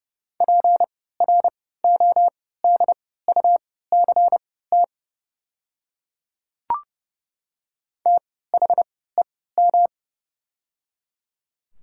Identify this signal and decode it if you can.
Morse code, heard as a keyed tone, about 30 wpm, spaced as for 17 wpm, 700 Hz. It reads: PRODUCT THEM
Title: 3x Character Spacing - Top 500 Words - 30wpm